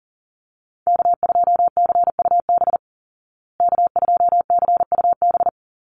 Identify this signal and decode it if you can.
Morse code, heard as a keyed tone, about 40 wpm, 700 Hz. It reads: K2CUB K2CUB
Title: Call Signs Repeated 3x - 40wpm